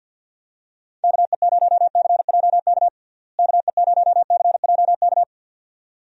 Morse code, heard as a keyed tone, about 52 wpm, 700 Hz. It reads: KE0CJK KE0CJK
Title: USA Call Signs - 6 Characters - Repeated 2x - 50wpm